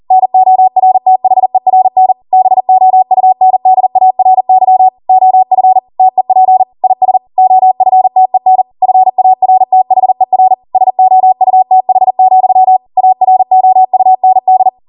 Morse code, heard as a keyed tone, about 40 wpm, 750 Hz. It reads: NORTHERN BOUNDARY OF TEP IS OFTEN FARTHER SOUTH, AROUND